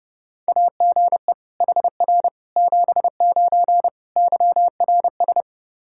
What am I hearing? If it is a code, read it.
Morse code, heard as a keyed tone, about 30 wpm, 700 Hz. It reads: AGE HR 79 YRS
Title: Practice QSO - Exchange 3 - More Info - 30wpm